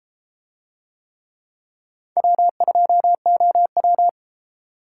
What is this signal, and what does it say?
Morse code, heard as a keyed tone, about 33 wpm, 700 Hz. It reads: W2OW